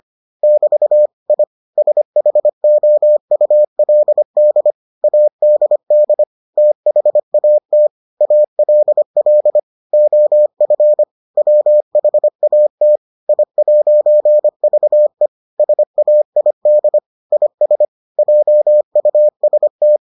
Morse code, heard as a keyed tone, about 25 wpm, 600 Hz. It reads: <BT> I SHOULD ADD THAT ALL OF WHAT I'VE SAID IS JUST